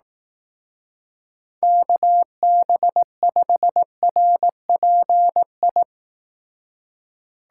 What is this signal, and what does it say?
Morse code, heard as a keyed tone, about 18 wpm, 700 Hz. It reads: KB5RPI
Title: Speed Racing - USA 2x3 Call Signs 15wpm